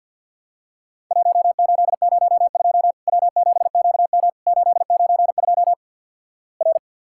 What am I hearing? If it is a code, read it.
Morse code, heard as a keyed tone, about 52 wpm, 700 Hz. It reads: 1802 W7QM 892 R
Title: Worked All Europe DX Contest - QTC Exchange - 50wpm